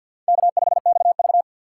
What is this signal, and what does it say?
Morse code, heard as a keyed tone, about 53 wpm, 700 Hz. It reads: K5XV
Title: USA Call Signs - 4 Characters - 2x Character Spacing - 50wpm